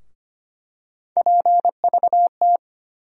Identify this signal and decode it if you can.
Morse code, heard as a keyed tone, about 25 wpm, 700 Hz. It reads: PVT
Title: USA Call Sign Postfixes - 3 Letters 25wpm